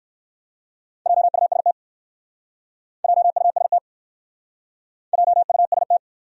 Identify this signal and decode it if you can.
Morse code, heard as a keyed tone, about 60 wpm, 700 Hz. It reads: JUST JUST JUST